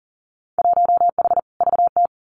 Morse code, heard as a keyed tone, about 39 wpm, 700 Hz. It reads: 1H VT
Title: Winter Field Day Exchanges 40wpm